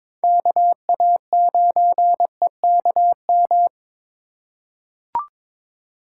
Morse code, heard as a keyed tone, about 22 wpm, 700 Hz. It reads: KA9EKM
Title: USA Call Signs - 6 Characters - 3x Character Spacing - 22wpm